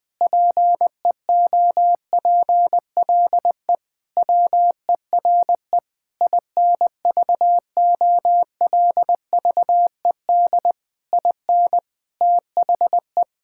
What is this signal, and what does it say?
Morse code, heard as a keyed tone, about 20 wpm, 700 Hz. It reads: PEOPLE WERE INVOLVED IN THE
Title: Sentences from Top 1000 Words - Review Part 5 - 20wpm